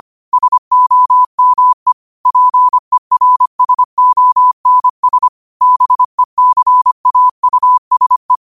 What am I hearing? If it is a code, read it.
Morse code, heard as a keyed tone, about 25 wpm, 1000 Hz. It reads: SOME PERSONS BECAUSE
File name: 25wpm